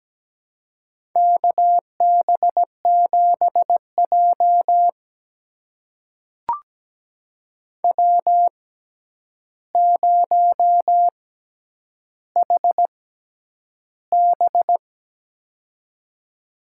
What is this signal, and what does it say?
Morse code, heard as a keyed tone, about 17 wpm, spaced as for 9 wpm, 700 Hz. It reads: KB7J W0HB
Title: USA Call Signs - 4 Characters - 6x Character Spacing - 17wpm